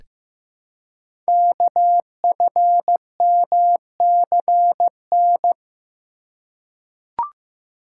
Morse code, heard as a keyed tone, about 15 wpm, 700 Hz. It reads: KFMCN
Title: Code Groups - 5 Letters - 15wpm